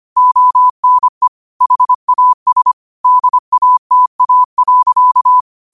Morse code, heard as a keyed tone, about 25 wpm, 1000 Hz. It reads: ONE HAS DATA.